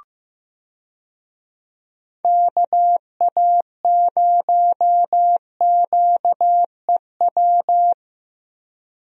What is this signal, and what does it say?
Morse code, heard as a keyed tone, about 15 wpm, 700 Hz. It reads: KA0QEW
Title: USA Call Signs - 6 Characters - Repeated 2x - 15wpm